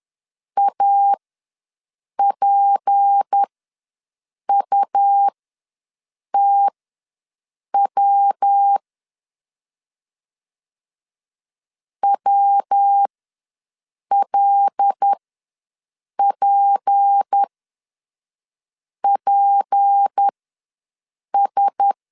Morse code, heard as a keyed tone, about 10 wpm, spaced as for 6 wpm, 800 Hz. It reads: APUTW WLPPS
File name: LEZIONI / 6 PKX0 / 6 P ws 10wpm 3751